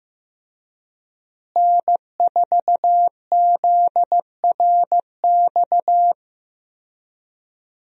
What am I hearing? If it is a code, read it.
Morse code, heard as a keyed tone, about 15 wpm, 700 Hz. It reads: N4ZRX
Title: USA Call Signs - 5 Characters - Repeated 3x - 15wpm